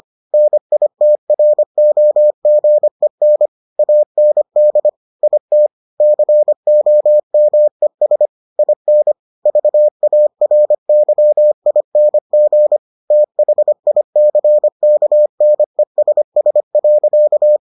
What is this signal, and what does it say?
Morse code, heard as a keyed tone, about 25 wpm, 600 Hz. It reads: NITROGEN AND IT COMES IN VARYING THICKNESS.